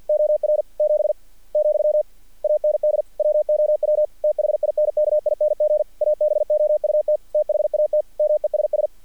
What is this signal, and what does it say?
Morse code, heard as a keyed tone, about 48 wpm, 600 Hz. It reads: OR 7 , AND WOW THINKING ABOUT THAT MESS
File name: archived / CRSnet